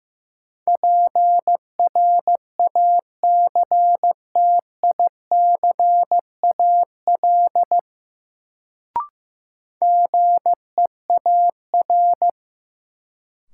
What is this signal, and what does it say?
Morse code, heard as a keyed tone, about 15 wpm, 700 Hz. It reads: PRACTICAL GEAR
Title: Sentences from Top 1400 Words - New Word Drill - 15wpm